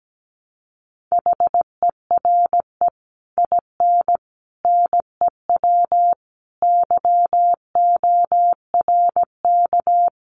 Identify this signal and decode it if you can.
Morse code, heard as a keyed tone, about 17 wpm, 700 Hz. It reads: HERE IN NEW YORK
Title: Sets of 5 Words Encore2 17wpm